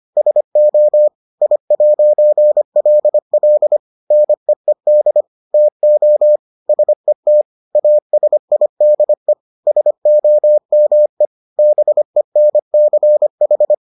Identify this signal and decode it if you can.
Morse code, heard as a keyed tone, about 25 wpm, 600 Hz. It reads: SO I'LL NEED TO SET ASIDE SOME BENCH